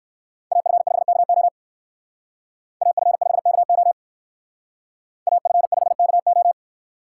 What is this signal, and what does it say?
Morse code, heard as a keyed tone, about 58 wpm, 700 Hz. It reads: AF5KO AF5KO AF5KO